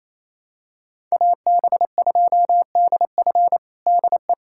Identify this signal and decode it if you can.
Morse code, heard as a keyed tone, about 28 wpm, 700 Hz. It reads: AB2DF DE
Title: USA Callsign DE Callsign 28wpm